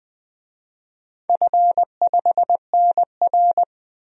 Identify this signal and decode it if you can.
Morse code, heard as a keyed tone, about 20 wpm, 700 Hz. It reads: F5NR
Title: DXCC-100 Call Signs - 4 Characters - 20wpm